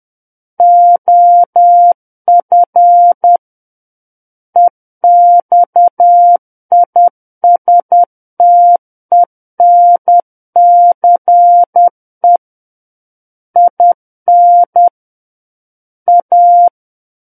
Morse code, heard as a keyed tone, about 10 wpm, 700 Hz. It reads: OF EXISTENCE IN A
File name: Book One / 01-Chapter 1 - 10 WPM